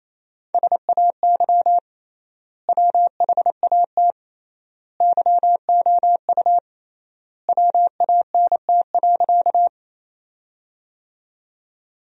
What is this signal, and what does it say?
Morse code, heard as a keyed tone, about 27 wpm, 700 Hz. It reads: SAY WHAT YOU WANT.
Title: 3x Word-Spacing - Sentences from Top 200 Words - 28wpm